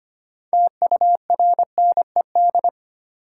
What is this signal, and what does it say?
Morse code, heard as a keyed tone, about 25 wpm, 700 Hz. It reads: TURNED
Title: Sets of 4 Words - New Word Drill - 25wpm